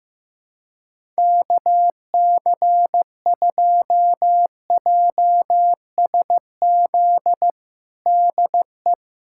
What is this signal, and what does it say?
Morse code, heard as a keyed tone, about 16 wpm, 700 Hz. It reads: KC2JSZ DE
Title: USA Callsign DE Callsign - 6 Characters - 15wpm